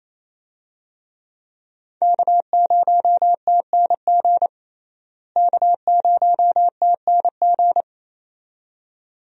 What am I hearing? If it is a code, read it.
Morse code, heard as a keyed tone, about 27 wpm, 700 Hz. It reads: K0TNG K0TNG
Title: USA Call Signs - 5 Characters - Repeated 2x - 28wpm